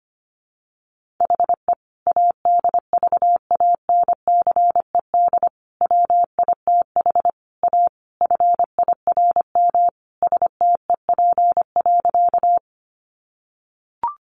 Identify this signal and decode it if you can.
Morse code, heard as a keyed tone, about 25 wpm, 700 Hz. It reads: HE ADVANCED WITH A FIRM STEP.
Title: Sentences from Top 1400 Words 25wpm